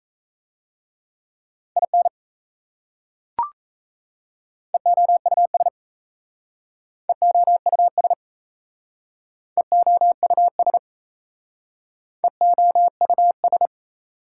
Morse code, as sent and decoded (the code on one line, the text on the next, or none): .. -. / . --- ..- ... / . --- ..- ... / . --- ..- ... / . --- ..- ...
IN EOUS EOUS EOUS EOUS